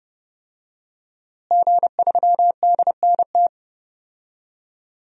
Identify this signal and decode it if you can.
Morse code, heard as a keyed tone, about 30 wpm, 700 Hz. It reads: G3DNT
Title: DXCC-100 Call Signs - 5 Characters - Repeated 3x - 30wpm